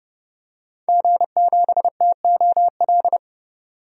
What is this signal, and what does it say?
Morse code, heard as a keyed tone, about 30 wpm, 700 Hz. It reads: G7TOL
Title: DXCC-100 Call Signs - 5 Characters - 30wpm